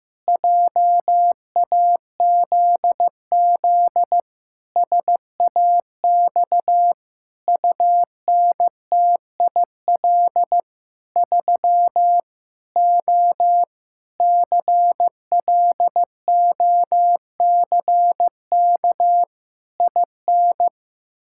Morse code, heard as a keyed tone, about 16 wpm, 700 Hz. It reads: JAZZ SAX UNTIL 3 O CLOCK IN
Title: Pangram Challenge v2 - 15wpm